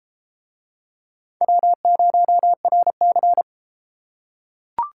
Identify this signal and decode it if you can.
Morse code, heard as a keyed tone, about 31 wpm, 700 Hz. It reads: W0RC